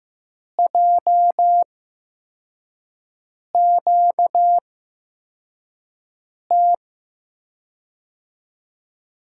Morse code, heard as a keyed tone, about 15 wpm, spaced as for 4 wpm, 700 Hz. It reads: JQT
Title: USA Call Signs - 6 Characters - 8x Character Spacing - 15wpm